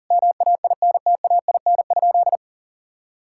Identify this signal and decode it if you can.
Morse code, heard as a keyed tone, about 40 wpm, 700 Hz. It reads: MAINTAIN?
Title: Sentences from Top 1000 Words - Review Part 5 - 40wpm